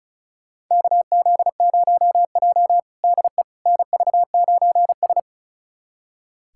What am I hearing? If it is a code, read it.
Morse code, heard as a keyed tone, about 36 wpm, 700 Hz. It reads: KZ0J DE NV9S
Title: USA Callsign DE Callsign - 4 Characters - 35wpm